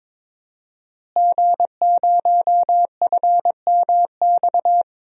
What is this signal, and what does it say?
Morse code, heard as a keyed tone, about 22 wpm, 700 Hz. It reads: G0FMX
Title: DXCC-100 Call Signs - 5 Characters - Repeated 2x - 22wpm